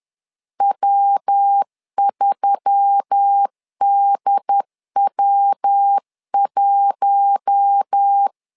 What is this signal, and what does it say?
Morse code, heard as a keyed tone, about 10 wpm, 800 Hz. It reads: W3DW1
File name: LEZIONI / 4 Q1S3 10wpm 2427